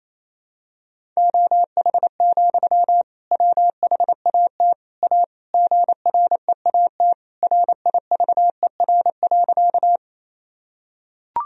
Morse code, heard as a keyed tone, about 28 wpm, 700 Hz. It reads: OH, WHAT A GREAT RIVER.
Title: Sentences from Top 1000 Words - Review Part 1 - 28wpm